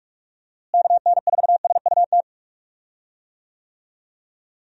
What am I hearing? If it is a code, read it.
Morse code, heard as a keyed tone, about 47 wpm, 700 Hz. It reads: KN4SUT
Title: USA Call Signs - 6 Characters - Repeated 2x - 45wpm